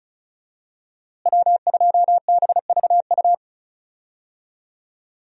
Morse code, heard as a keyed tone, about 36 wpm, 700 Hz. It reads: W2BVU
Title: USA Call Signs - 5 Characters - Repeated 2x - 35wpm